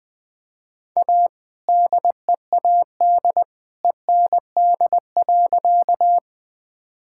Morse code, heard as a keyed tone, about 20 wpm, 700 Hz. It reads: A DEAD END.
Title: All-Age Appropriate Jokes - 20wpm